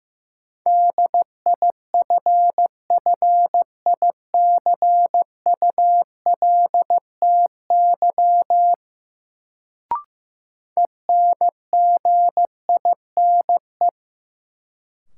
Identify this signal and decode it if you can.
Morse code, heard as a keyed tone, about 15 wpm, 700 Hz. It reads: DIFFICULTY ENGINE
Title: Sentences from Top 1500 Words - New Word Drill Alt - 15wpm